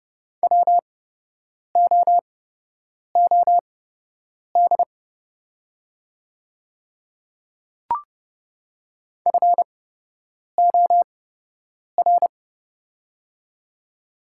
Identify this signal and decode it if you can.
Morse code, heard as a keyed tone, about 31 wpm, spaced as for 8 wpm, 700 Hz. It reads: WOOD FOR
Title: 8x Character Spacing - Top 300 Words - 30wpm